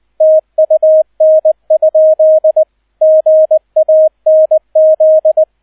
Morse code, heard as a keyed tone, about 20 wpm, 630 Hz, with radio noise. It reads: TUN? GANZ